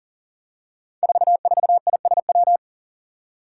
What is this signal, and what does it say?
Morse code, heard as a keyed tone, about 38 wpm, 700 Hz. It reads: 44ISW
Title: Five Challenging Letter-Number Combinations 40wpm